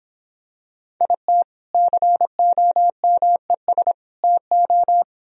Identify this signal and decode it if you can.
Morse code, heard as a keyed tone, about 26 wpm, 700 Hz. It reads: IT COMES TO